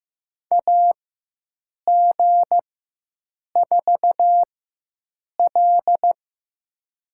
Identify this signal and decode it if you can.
Morse code, heard as a keyed tone, about 15 wpm, spaced as for 7 wpm, 700 Hz. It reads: AG4L
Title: USA Call Signs - 4 Characters - 4x Character Spacing - 15wpm